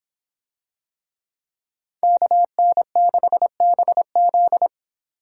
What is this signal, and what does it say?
Morse code, heard as a keyed tone, about 25 wpm, 700 Hz. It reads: KN6BZ